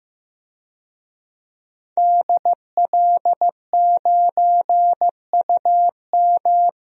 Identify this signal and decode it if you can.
Morse code, heard as a keyed tone, about 15 wpm, 700 Hz. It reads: DL9UM
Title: DXCC-100 Call Signs - 5 Characters - 15wpm